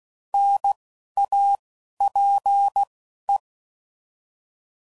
Morse code, heard as a keyed tone, about 16 wpm, 800 Hz. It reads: N A P E
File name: Prefixes / By First Letter / Ns